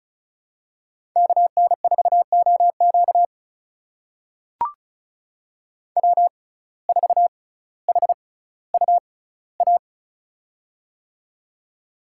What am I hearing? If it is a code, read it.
Morse code, heard as a keyed tone, about 36 wpm, spaced as for 18 wpm, 700 Hz. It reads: KN4OQ W4HUA
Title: USA Call Signs - 5 Characters - 6x Character Spacing - 35wpm